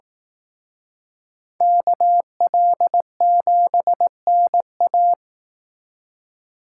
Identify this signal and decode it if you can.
Morse code, heard as a keyed tone, about 18 wpm, 700 Hz. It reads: KL7NA
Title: Speed Racing - USA 2x2 Call Signs 15wpm